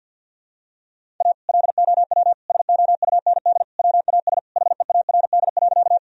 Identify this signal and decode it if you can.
Morse code, heard as a keyed tone, about 52 wpm, 700 Hz. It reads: A LOW SOUND WAS HEARD.
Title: Sentences from Top 1000 Words - Review Part 1 - 50wpm